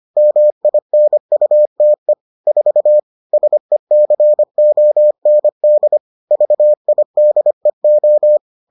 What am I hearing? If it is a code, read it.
Morse code, heard as a keyed tone, about 25 wpm, 600 Hz. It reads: MINUTE 4 SECOND VIDEO